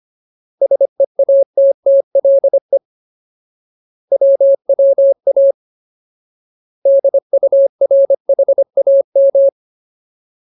Morse code, heard as a keyed tone, about 25 wpm, 550 Hz. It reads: SEATTLE WWA DURHAM
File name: Ultimate Word List / 25WPM Ultimate